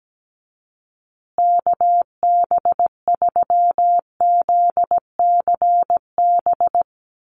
Morse code, heard as a keyed tone, about 17 wpm, 700 Hz. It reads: KB3ZCB
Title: USA Call Signs Repeated 2x 17wpm